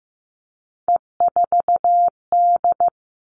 Morse code, heard as a keyed tone, about 15 wpm, 700 Hz. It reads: E4D
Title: All Cut Numbers - 3-Digit 15wpm